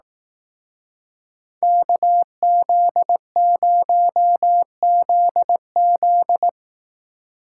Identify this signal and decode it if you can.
Morse code, heard as a keyed tone, about 18 wpm, 700 Hz. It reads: KZ0ZZ
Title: Speed Racing - USA 2x2 Call Signs 15wpm